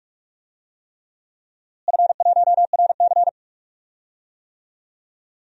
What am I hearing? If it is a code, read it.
Morse code, heard as a keyed tone, about 41 wpm, 700 Hz. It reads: F1RC
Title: DXCC-100 Call Signs - 4 Characters - Repeated 3x - 45wpm